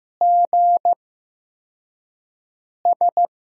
Morse code, heard as a keyed tone, about 15 wpm, 700 Hz. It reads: G S
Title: USA Call Signs - 5 Characters - 8x Character Spacing - 15wpm